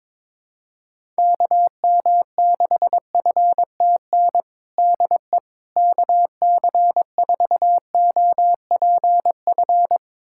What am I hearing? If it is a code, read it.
Morse code, heard as a keyed tone, about 22 wpm, 700 Hz. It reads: KM6FTN DE KC4OPF
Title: USA Callsign DE Callsign 22wpm